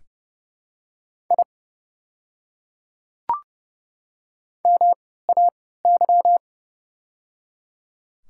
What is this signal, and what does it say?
Morse code, heard as a keyed tone, about 29 wpm, spaced as for 17 wpm, 700 Hz. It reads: I MAY
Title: 3x Character Spacing - Top 200 Words - 30wpm